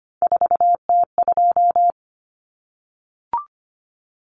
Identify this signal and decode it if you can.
Morse code, heard as a keyed tone, about 25 wpm, 700 Hz. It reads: 4T2
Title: Common Cut Numbers - 3-Digit 25wpm